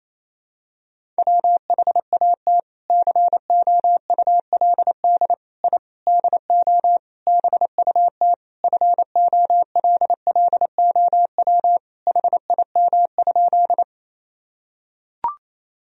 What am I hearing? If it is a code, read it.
Morse code, heard as a keyed tone, about 28 wpm, 700 Hz. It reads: WHAT COULD I DO BUT FOLLOW HIM?
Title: Sentences from Top 1000 Words - Review Part 1 - 28wpm